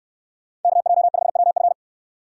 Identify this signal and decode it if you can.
Morse code, heard as a keyed tone, about 69 wpm, 700 Hz. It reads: KJ5WF